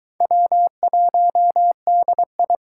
Morse code, heard as a keyed tone, about 24 wpm, 700 Hz. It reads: W1DI